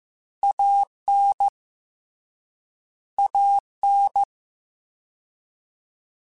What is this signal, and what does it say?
Morse code, heard as a keyed tone, about 15 wpm, 800 Hz. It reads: AN AN
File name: Short Runs / Short Runs-mp3s / ShortContrasts @13